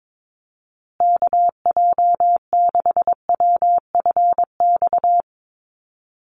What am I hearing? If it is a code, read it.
Morse code, heard as a keyed tone, about 22 wpm, 700 Hz. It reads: KJ6WFX
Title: USA Call Signs Repeated 3x 22wpm